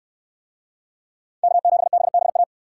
Morse code, heard as a keyed tone, about 63 wpm, 700 Hz. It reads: K7BBA